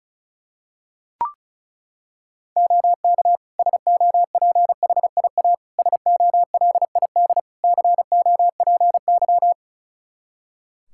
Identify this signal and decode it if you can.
Morse code, heard as a keyed tone, about 34 wpm, 700 Hz. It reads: OK SOPHIA SOLID COPY
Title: Practice QSO - Exchange 3 - More Info - 35wpm